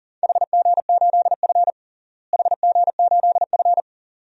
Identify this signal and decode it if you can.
Morse code, heard as a keyed tone, about 40 wpm, 700 Hz. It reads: HG8F HG8F
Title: DXCC-100 Call Signs - 4 Characters - Repeated 2x - 40wpm